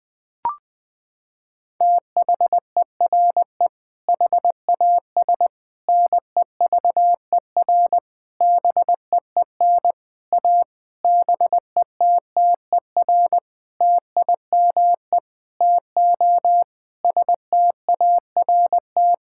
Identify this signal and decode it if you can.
Morse code, heard as a keyed tone, about 20 wpm, 700 Hz. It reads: THERE HAS NEVER BEEN A BETTER TIME TO START
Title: Sentences with 7-Word Phrases - 20wpm